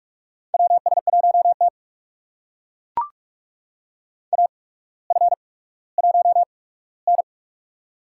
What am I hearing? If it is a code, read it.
Morse code, heard as a keyed tone, about 43 wpm, spaced as for 19 wpm, 700 Hz. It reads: WS1T AF1N
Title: USA Call Signs - 4 Characters - 8x Character Spacing - 45wpm